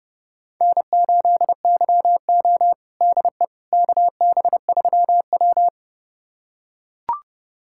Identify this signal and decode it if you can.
Morse code, heard as a keyed tone, about 30 wpm, 700 Hz. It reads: N8YO DE KB3W
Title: USA Callsign DE Callsign - 4 Characters - 30wpm